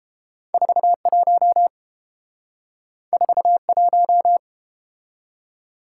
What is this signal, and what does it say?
Morse code, heard as a keyed tone, about 32 wpm, 700 Hz. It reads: 41 41